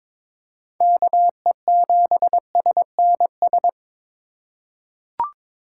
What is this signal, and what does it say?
Morse code, heard as a keyed tone, about 22 wpm, 700 Hz. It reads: KE7SNS
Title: USA Call Signs - 6 Characters - 4x Character Spacing - 22wpm